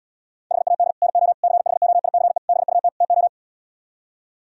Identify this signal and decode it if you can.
Morse code, heard as a keyed tone, about 77 wpm, 690 Hz. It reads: HAD NO CHOICE BUT TO